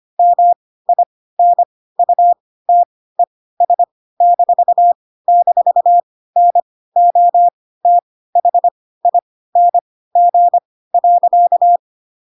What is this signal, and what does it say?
Morse code, heard as a keyed tone, about 25 wpm, spaced as for 16 wpm, 700 Hz. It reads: MINUTES--NOTHING.
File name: Pinocchio / Pinocchio ch1 10 25 ik1hge